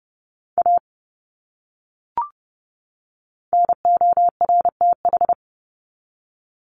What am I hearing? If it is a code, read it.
Morse code, heard as a keyed tone, about 30 wpm, 700 Hz. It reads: A NORTH
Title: Top 300 Words 30wpm